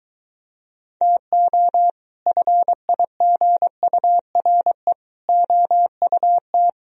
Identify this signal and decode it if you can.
Morse code, heard as a keyed tone, about 23 wpm, 700 Hz. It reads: TO FIGURE OUT